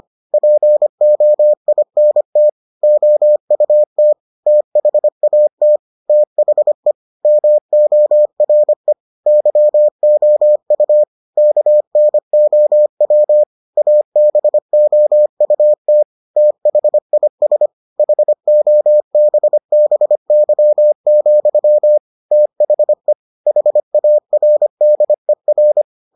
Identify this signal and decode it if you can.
Morse code, heard as a keyed tone, about 25 wpm, 600 Hz. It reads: POINT OUT THAT THE MORE YOU KNOW ABOUT THIS HOBBY, THE HARDER